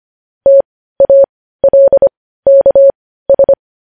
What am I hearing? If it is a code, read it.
Morse code, heard as a keyed tone, about 25 wpm, spaced as for 15 wpm, 550 Hz. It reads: TALKS
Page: news headlines in morse code at 15 wpm